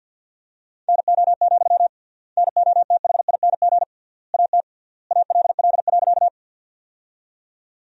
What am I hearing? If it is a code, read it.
Morse code, heard as a keyed tone, about 47 wpm, 700 Hz. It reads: NO, NOTHING AT ALL.
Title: 3x Word-Spacing - Sentences from Top 500 Words - 50wpm